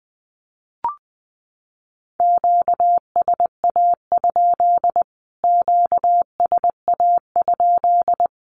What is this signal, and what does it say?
Morse code, heard as a keyed tone, about 20 wpm, 700 Hz. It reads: QSA? QSA?
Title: Q-Codes Repeated 3x 20wpm